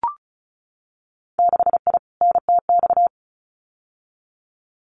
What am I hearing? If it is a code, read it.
Morse code, heard as a keyed tone, about 36 wpm, 700 Hz. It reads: 6I NTX
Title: Winter Field Day Exchanges 35wpm